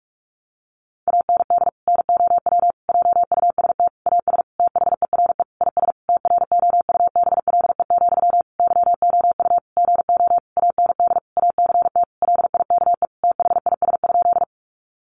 Morse code, heard as a keyed tone, about 44 wpm, 700 Hz. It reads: AND NOW JUST AS THERE IS TROUBLE, YOU GO AND ACT LIKE THIS?
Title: Sentences from Top 1200 Words 45wpm